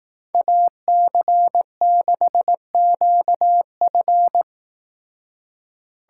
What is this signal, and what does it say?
Morse code, heard as a keyed tone, about 18 wpm, 700 Hz. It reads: AC6QF
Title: Speed Racing - USA 2x2 Call Signs 15wpm